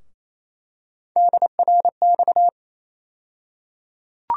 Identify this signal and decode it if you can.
Morse code, heard as a keyed tone, about 27 wpm, 700 Hz. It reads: DRX
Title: USA Call Sign Postfixes - 3 Letters 28wpm